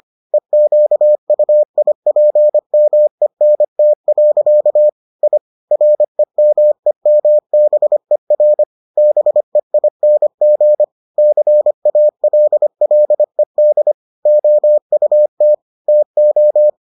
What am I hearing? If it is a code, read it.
Morse code, heard as a keyed tone, about 25 wpm, 600 Hz. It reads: EQUIPMENT. I REMEMBER BEING CALLED OUT TO